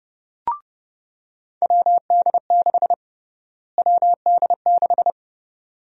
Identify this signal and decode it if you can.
Morse code, heard as a keyed tone, about 32 wpm, 700 Hz. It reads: WD6 WD6
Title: USA Call Sign Prefixes - Repeated 2x - 30wpm